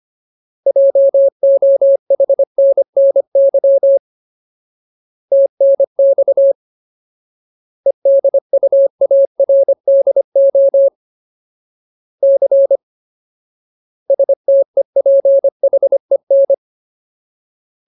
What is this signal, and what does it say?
Morse code, heard as a keyed tone, about 25 wpm, 550 Hz. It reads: JOHNNY TNX EDUARDO C STEPHEN
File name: Ultimate Word List / 25WPM Ultimate